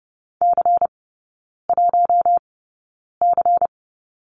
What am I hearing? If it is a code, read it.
Morse code, heard as a keyed tone, about 30 wpm, spaced as for 9 wpm, 700 Hz. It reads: C1C
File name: Lesson 16e - Letter c - Three Characters 30wpm